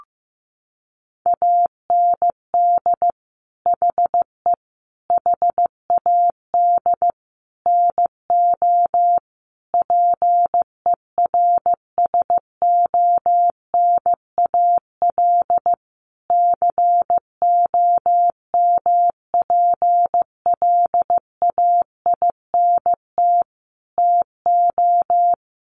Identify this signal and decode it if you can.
Morse code, heard as a keyed tone, about 15 wpm, 700 Hz. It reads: AND HE HAD NO PERSONAL COMPLAINT TO
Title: Sentences from Top 1400 Words 15wpm